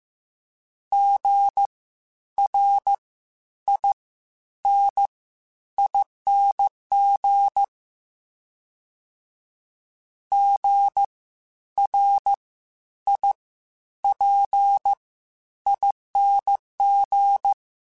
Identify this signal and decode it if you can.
Morse code, heard as a keyed tone, about 15 wpm, spaced as for 10 wpm, 800 Hz. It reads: GRINING GRIPING
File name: Spaced Alphas / Spaced Alphas mp3s / Gs sp @13